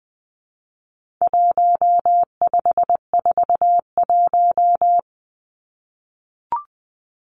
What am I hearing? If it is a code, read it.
Morse code, heard as a keyed tone, about 20 wpm, 700 Hz. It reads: 1541
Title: Numbers - 4-Digit 20wpm